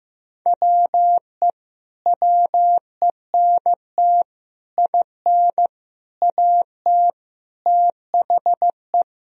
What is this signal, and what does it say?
Morse code, heard as a keyed tone, about 15 wpm, 700 Hz. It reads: WE WENT IN AT THE
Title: Sentences from Top 1000 Words - Review Part 2 - 15wpm